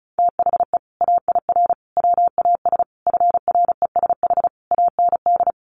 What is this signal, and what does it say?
Morse code, heard as a keyed tone, about 34 wpm, 700 Hz. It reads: THE AIR WAS FRESH AND
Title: Sentences from Top 1200 Words 35wpm